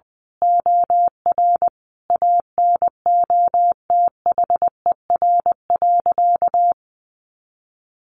Morse code, heard as a keyed tone, about 20 wpm, 700 Hz. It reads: OR ANOTHER.
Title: Sentences from Top 1400 Words 20wpm